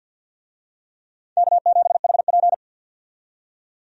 Morse code, heard as a keyed tone, about 53 wpm, 700 Hz. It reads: K7HP
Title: USA Call Signs - 4 Characters - 50wpm